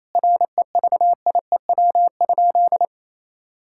Morse code, heard as a keyed tone, about 28 wpm, 700 Hz. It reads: REVIEW?
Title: Sentences from Top 1000 Words - Review Part 6 - 28wpm